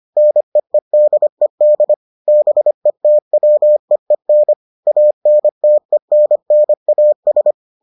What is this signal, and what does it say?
Morse code, heard as a keyed tone, about 25 wpm, 600 Hz. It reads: NEEDED BETWEEN ANTENNAS